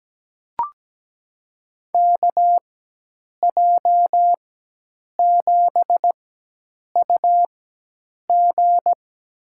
Morse code, heard as a keyed tone, about 17 wpm, spaced as for 8 wpm, 700 Hz. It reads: KJ7UG
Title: USA Call Signs - 5 Characters - 4x Character Spacing - 17wpm